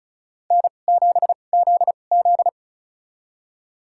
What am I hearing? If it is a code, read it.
Morse code, heard as a keyed tone, about 36 wpm, spaced as for 25 wpm, 700 Hz. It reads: N7ZZ
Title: USA Call Signs - 2x Character Spacing - 35wpm